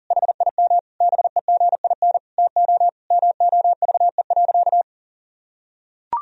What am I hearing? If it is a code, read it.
Morse code, heard as a keyed tone, about 40 wpm, 700 Hz. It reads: HIM BEGIN TO MOVE.
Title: Sentences from Top 1000 Words - Review Part 2 - 40wpm